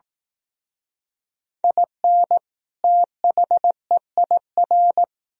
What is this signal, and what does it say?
Morse code, heard as a keyed tone, about 19 wpm, 700 Hz. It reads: IN THEIR
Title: Speed-Racing - Sets of 2 Words 15wpm